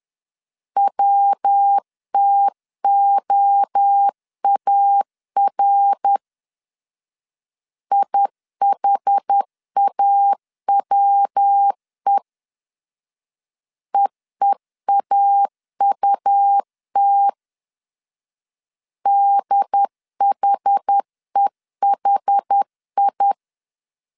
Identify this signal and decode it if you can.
Morse code, heard as a keyed tone, about 10 wpm, 800 Hz. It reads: WTOAR IHAWE EEAUT DHEHI
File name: CW / DET / 1 3 1500C S C 10wpm 3335